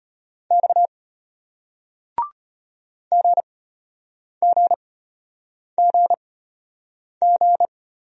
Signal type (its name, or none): Morse code